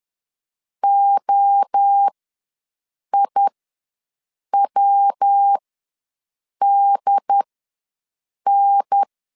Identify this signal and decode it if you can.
Morse code, heard as a keyed tone, about 10 wpm, spaced as for 6 wpm, 800 Hz. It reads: OIWDN
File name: LEZIONI / 5 LBVC / 5 L ws 10wpm 3636